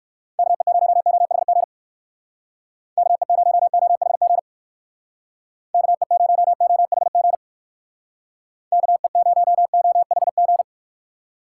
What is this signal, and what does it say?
Morse code, heard as a keyed tone, about 59 wpm, 700 Hz. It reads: KE0OHG KE0OHG KE0OHG KE0OHG